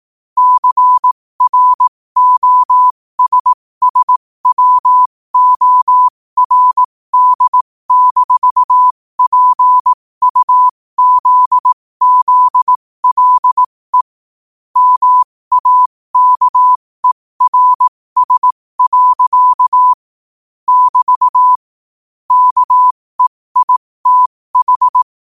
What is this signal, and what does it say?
Morse code, heard as a keyed tone, about 18 wpm, spaced as for 15 wpm, 1000 Hz. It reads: CROSSWORD-PUZZLE MAKERS. <BT> KEITH
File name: qotdfiles / 15wpm